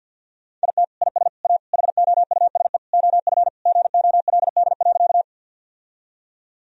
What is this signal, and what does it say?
Morse code, heard as a keyed tone, about 52 wpm, 700 Hz. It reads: IT IS A HOUSE OF GOLD.
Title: Sentences from Top 1000 Words - Review Part 3 - 50wpm